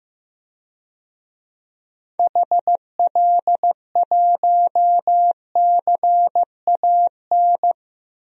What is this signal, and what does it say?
Morse code, heard as a keyed tone, about 15 wpm, 700 Hz. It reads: HL1CAN
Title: DXCC-100 Call Signs - 6 Characters - 15wpm